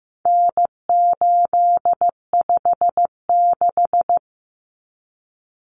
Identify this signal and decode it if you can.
Morse code, heard as a keyed tone, about 15 wpm, 700 Hz. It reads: N856
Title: Common Cut Numbers - 4-Digit 15wpm